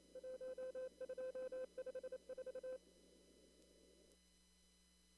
Morse code, heard as a keyed tone, about 28 wpm, 520 Hz. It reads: 1254
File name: CW / kt5j wpx cw-op-k5pi